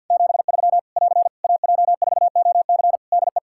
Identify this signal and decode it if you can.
Morse code, heard as a keyed tone, about 50 wpm, 700 Hz. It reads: 73 <AR> AJ4OC DE
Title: Practice QSO - Exchange 4 - Ending - 50wpm